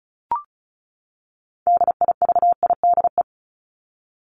Morse code, heard as a keyed tone, about 34 wpm, 700 Hz. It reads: DIVIDE
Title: ED Suffix - Top 100 Words with Base Words 35wpm